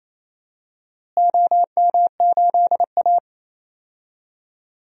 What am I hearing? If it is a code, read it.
Morse code, heard as a keyed tone, about 28 wpm, 700 Hz. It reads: OM8A
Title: DXCC-100 Call Signs - 4 Characters - Repeated 2x - 28wpm